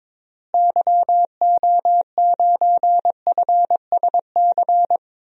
Morse code, heard as a keyed tone, about 22 wpm, 700 Hz. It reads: YO9FSC
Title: DXCC-100 Call Signs - 6 Characters - Repeated 2x - 22wpm